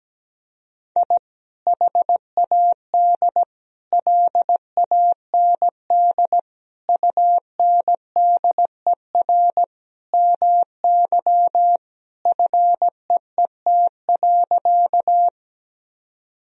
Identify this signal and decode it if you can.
Morse code, heard as a keyed tone, about 17 wpm, 700 Hz. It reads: I HAD LAND UNDER MY FEET.
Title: Sentences from Top 1000 Words - Review Part 1 - 17wpm